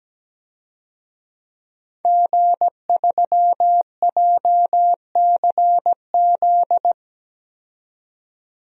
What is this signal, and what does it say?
Morse code, heard as a keyed tone, about 17 wpm, 700 Hz. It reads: G3JCZ
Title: DXCC-100 Call Signs - 5 Characters - 17wpm